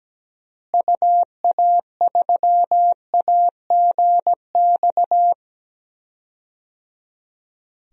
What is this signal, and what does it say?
Morse code, heard as a keyed tone, about 17 wpm, 700 Hz. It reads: UA3AGX
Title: DXCC-100 Call Signs - 6 Characters - Repeated 3x - 17wpm